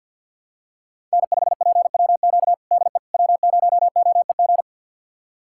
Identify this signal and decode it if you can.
Morse code, heard as a keyed tone, about 51 wpm, 700 Hz. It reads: N5PWQ DE W0OEG
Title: USA Callsign DE Callsign - 5 Characters - 50wpm